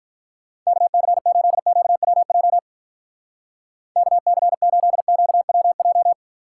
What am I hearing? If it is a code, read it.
Morse code, heard as a keyed tone, about 46 wpm, 700 Hz. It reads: KC8QWJ KC8QWJ